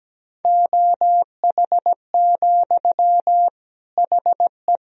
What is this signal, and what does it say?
Morse code, heard as a keyed tone, about 17 wpm, 700 Hz. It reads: OH, HE
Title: Sentences from Top 500 Words and Abbreviations 17wpm